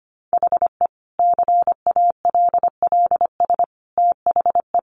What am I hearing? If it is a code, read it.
Morse code, heard as a keyed tone, about 25 wpm, 700 Hz. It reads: HE CALLS THE
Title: Additional Easy Phrases 25wpm